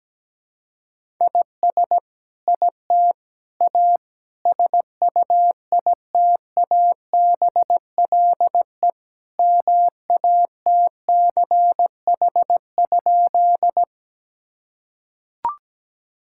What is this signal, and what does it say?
Morse code, heard as a keyed tone, about 17 wpm, 700 Hz. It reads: IS IT A SUITABLE MATCH?
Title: Sentences from Top 1600 Words 17wpm